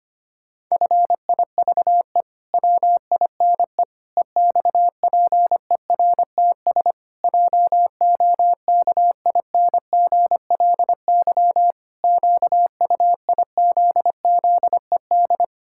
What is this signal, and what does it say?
Morse code, heard as a keyed tone, about 25 wpm, 700 Hz. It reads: FIVE WINE EXPERTS JOKINGLY QUIZZED
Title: Pangram Challenge v2 - 25wpm